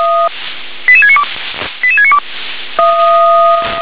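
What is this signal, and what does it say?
Five-tone selective call (ZVEI), heard as quick runs of five tones, tones of about 70 ms. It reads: 88692 88692
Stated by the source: ZVEI five-tone call 88797 received on SDR.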